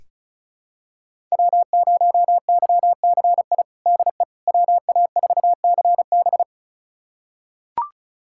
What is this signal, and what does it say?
Morse code, heard as a keyed tone, about 36 wpm, 700 Hz. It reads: W0YCI DE WA4CB
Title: USA Callsign DE Callsign - 5 Characters - 35wpm